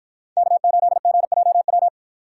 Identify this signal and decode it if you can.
Morse code, heard as a keyed tone, about 53 wpm, 700 Hz. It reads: K8GJW